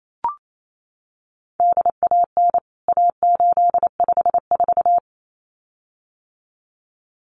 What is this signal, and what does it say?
Morse code, heard as a keyed tone, about 28 wpm, 700 Hz. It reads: DAN A854
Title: CWT Exchange 28wpm